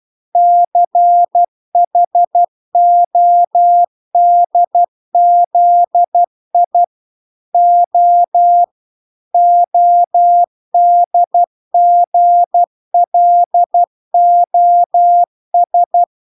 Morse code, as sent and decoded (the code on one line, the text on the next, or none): -.-. .... --- -.. --.. .. / --- / --- -.. --. .-.. --- ...
CHODZI O ODGLOS